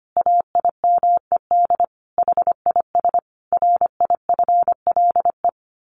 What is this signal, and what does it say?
Morse code, heard as a keyed tone, about 25 wpm, 700 Hz. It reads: AIMED HIS RIFLE
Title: A Sound Of Thunder 25wpm